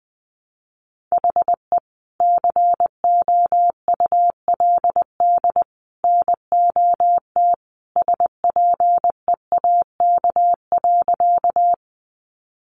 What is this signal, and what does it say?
Morse code, heard as a keyed tone, about 20 wpm, 700 Hz. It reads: HE COULD NOT SPEAK.
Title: Sentences from Top 1100 Words 20wpm